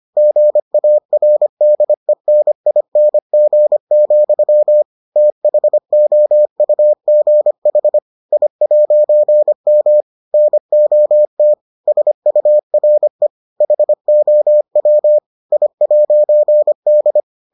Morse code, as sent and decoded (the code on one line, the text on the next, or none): --. .- .-. -.. -. .. -. --. --..-- / - .... --- ..- --. .... / .. .----. -- / -. --- - / ... ..- .-. / .... --- .-- / .. .----. -..
GARDENING, THOUGH I'M NOT SURE HOW I'D